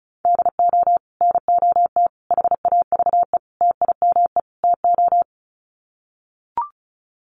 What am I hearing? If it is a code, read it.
Morse code, heard as a keyed tone, about 34 wpm, 700 Hz. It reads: DO NOT HAVE TIME TO
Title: Sets of 5 Words 35wpm